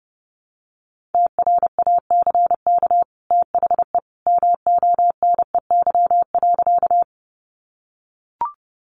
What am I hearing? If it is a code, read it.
Morse code, heard as a keyed tone, about 30 wpm, 700 Hz. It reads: TRACK THE MONEY.
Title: Sentences from Top 1100 Words 30wpm